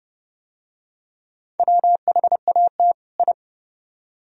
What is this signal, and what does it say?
Morse code, heard as a keyed tone, about 27 wpm, 700 Hz. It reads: WHAT I